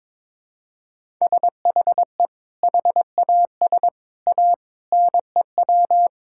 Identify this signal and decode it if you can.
Morse code, heard as a keyed tone, about 22 wpm, 700 Hz. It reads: SHE HAS A NEW
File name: practice / 22wpm / Sentences with 2-Word Phrases - 22wpm